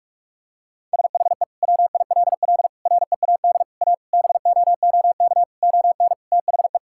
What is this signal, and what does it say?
Morse code, heard as a keyed tone, about 46 wpm, 700 Hz. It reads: SHE WILL READ A BOOK ON THE
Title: Sentences from Top 1000 Words - Review Part 6 - 45wpm